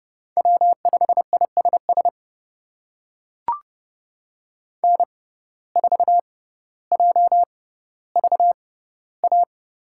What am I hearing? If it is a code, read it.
Morse code, heard as a keyed tone, about 30 wpm, spaced as for 15 wpm, 700 Hz. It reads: W5ISS N4JVA
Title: USA Call Signs - 5 Characters - 6x Character Spacing - 30wpm